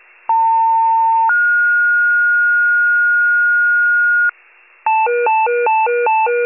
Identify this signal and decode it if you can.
Two-tone page, heard as a one-second tone, then a three-second tone, about 900 then 1430 Hz, with noise. It is heard from 0.3 to 4.3 s.